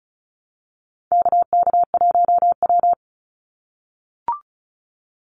Morse code, heard as a keyed tone, about 36 wpm, 700 Hz. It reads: KK1W
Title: NA SOTA Activator Callsigns 35wpm